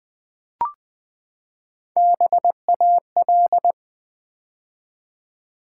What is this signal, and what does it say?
Morse code, heard as a keyed tone, about 20 wpm, 700 Hz. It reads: BAL
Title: Build A Word - 3 Letters At A Time - 6-Letter Words - 20wpm